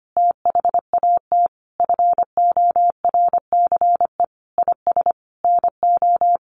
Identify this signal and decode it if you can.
Morse code, heard as a keyed tone, about 25 wpm, 700 Hz. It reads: THAT FORCE IS NO
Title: Sentences from Top 1400 Words 25wpm